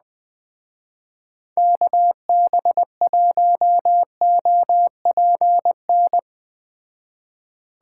Morse code, heard as a keyed tone, about 20 wpm, 700 Hz. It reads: KB1OPN
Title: Speed Racing - USA 2x3 Call Signs 17wpm